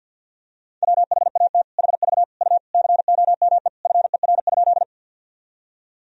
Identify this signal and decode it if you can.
Morse code, heard as a keyed tone, about 50 wpm, 700 Hz. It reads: WHAT HV U COME FER?
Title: Sentences from Top 100 Words and Abbreviations 50wpm